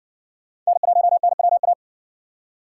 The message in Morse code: N1NWA

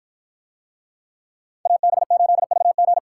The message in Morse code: AB8VG

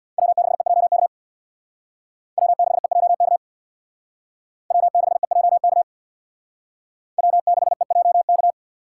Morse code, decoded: W6EJK W6EJK W6EJK W6EJK

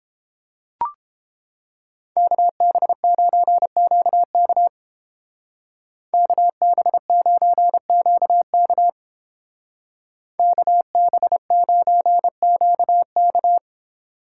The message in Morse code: KB9QK KB9QK KB9QK

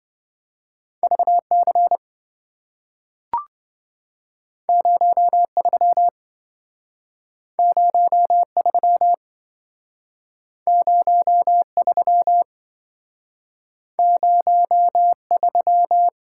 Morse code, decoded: VC 03 03 03 03

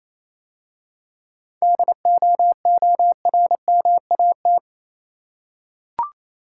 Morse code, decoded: DOORMAT

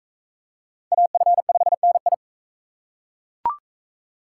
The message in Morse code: AF5NI

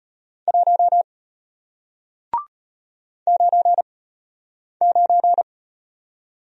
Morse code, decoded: 1 9 9

